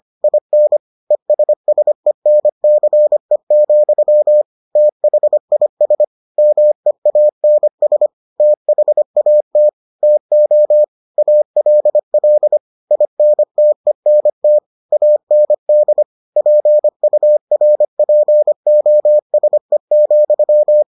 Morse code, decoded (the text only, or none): IN ESSENCE, THIS MEANS THAT TO ALL INTENT AND PURPOSE,